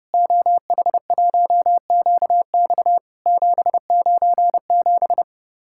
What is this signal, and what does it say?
Morse code, heard as a keyed tone, about 32 wpm, 700 Hz. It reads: OH1QX 797
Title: Worked All Europe DX Contest - QTC Exchange - 30wpm